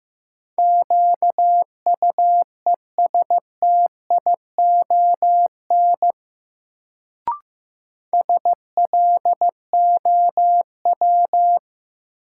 Morse code, heard as a keyed tone, about 15 wpm, 700 Hz. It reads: QUESTION SLOW
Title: Sentences from Top 400 Words - New Word Drill - 15wpm